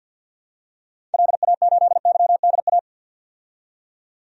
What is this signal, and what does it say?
Morse code, heard as a keyed tone, about 53 wpm, 700 Hz. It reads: LA8YDA
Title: DXCC-100 Call Signs - 6 Characters - 50wpm